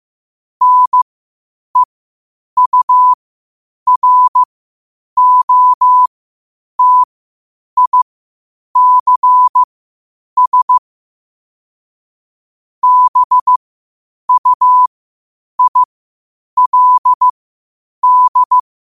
Morse code, heard as a keyed tone, about 15 wpm, spaced as for 8 wpm, 1000 Hz. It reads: NEUROTICS BUILD
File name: qotdfiles / 8wpm